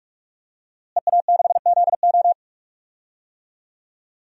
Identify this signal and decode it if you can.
Morse code, heard as a keyed tone, about 46 wpm, 700 Hz. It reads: EA6ZO
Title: DXCC-100 Call Signs - 5 Characters - Repeated 3x - 45wpm